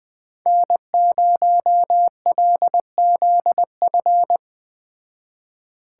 Morse code, heard as a keyed tone, about 20 wpm, 700 Hz. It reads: N0LZF
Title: Speed Racing - USA 1x3 Call Signs 17wpm